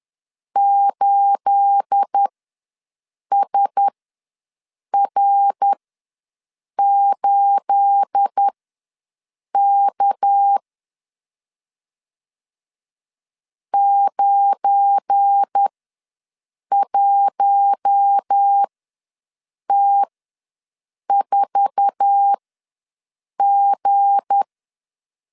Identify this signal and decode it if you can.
Morse code, heard as a keyed tone, about 10 wpm, spaced as for 6 wpm, 800 Hz. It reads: 8SR8K 91T4G
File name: LEZIONI / 9 28 ws 10wpm 4124